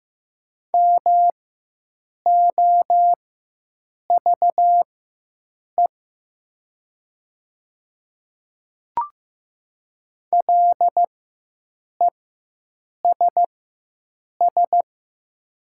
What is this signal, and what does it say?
Morse code, heard as a keyed tone, about 15 wpm, spaced as for 7 wpm, 700 Hz. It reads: MOVE LESS
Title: 4x Character Spacing - Top 500 Words - 15wpm